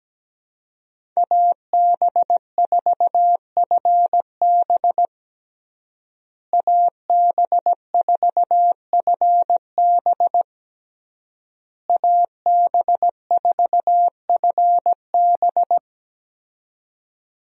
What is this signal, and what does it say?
Morse code, heard as a keyed tone, about 17 wpm, 700 Hz. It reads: AB4FB AB4FB AB4FB
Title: USA Call Signs - 5 Characters - Repeated 3x - 17wpm